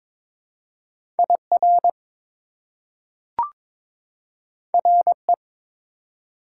Morse code, heard as a keyed tone, about 21 wpm, 700 Hz. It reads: IR RE
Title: Prefixes and Suffixes 22wpm